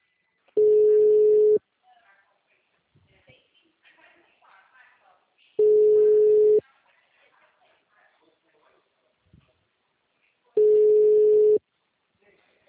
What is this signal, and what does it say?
Ringback tone (Europe, 425 Hz), with speech.
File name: out- -2019- - - .48003